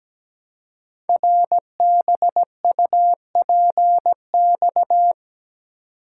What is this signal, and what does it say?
Morse code, heard as a keyed tone, about 17 wpm, 700 Hz. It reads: RBUPX
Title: Code Groups - 5 Letters - 17wpm